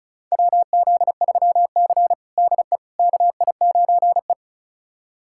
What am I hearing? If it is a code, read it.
Morse code, heard as a keyed tone, about 34 wpm, 700 Hz. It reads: WZ3C DE KI9E
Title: USA Callsign DE Callsign - 4 Characters - 35wpm